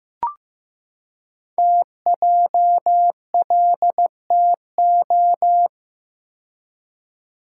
Morse code, heard as a keyed tone, about 15 wpm, 700 Hz. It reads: TJLTO
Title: Code Groups - 5 Letters - 15wpm